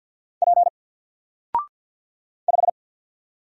Morse code, heard as a keyed tone, about 50 wpm, 700 Hz. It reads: P 5
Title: Single Letter-Number Alt 50wpm